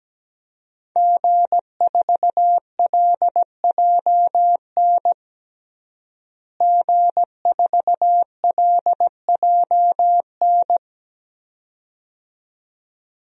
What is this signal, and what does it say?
Morse code, heard as a keyed tone, about 17 wpm, 700 Hz. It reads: G4LJN G4LJN
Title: DXCC-100 Call Signs - 5 Characters - Repeated 2x - 17wpm